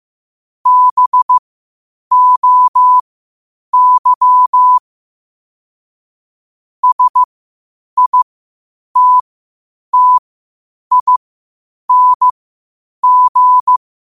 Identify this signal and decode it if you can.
Morse code, heard as a keyed tone, about 15 wpm, spaced as for 8 wpm, 1000 Hz. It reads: BOY SITTING
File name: qotdfiles / 8wpm